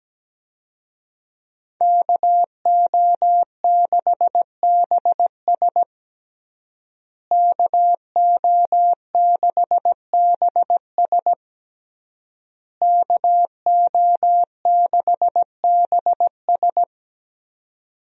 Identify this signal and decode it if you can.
Morse code, heard as a keyed tone, about 17 wpm, 700 Hz. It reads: KO6BS KO6BS KO6BS
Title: USA Call Signs - 5 Characters - Repeated 3x - 17wpm